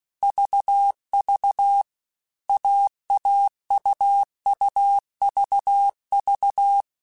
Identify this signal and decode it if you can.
Morse code, heard as a keyed tone, about 16 wpm, 800 Hz. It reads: VV AAUUVV